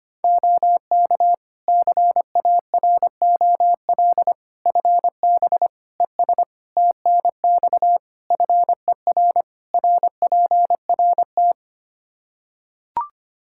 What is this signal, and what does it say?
Morse code, heard as a keyed tone, about 25 wpm, 700 Hz. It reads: OK CAROL FB ES TNX FER RPRT
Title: K1USN SST QSO Practice 25wpm